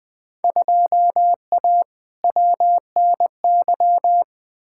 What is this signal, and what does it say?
Morse code, heard as a keyed tone, about 20 wpm, 700 Hz. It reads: 2A WNY
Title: Field Day Exchanges - 20wpm